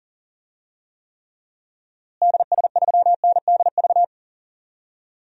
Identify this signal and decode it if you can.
Morse code, heard as a keyed tone, about 40 wpm, 700 Hz. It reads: DS3NDV
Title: DXCC-100 Call Signs - 6 Characters - Repeated 2x - 40wpm